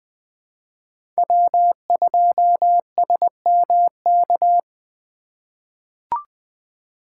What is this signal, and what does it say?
Morse code, heard as a keyed tone, about 20 wpm, 700 Hz. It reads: W2SMK
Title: USA Call Signs - 5 Characters - 2x Character Spacing - 20wpm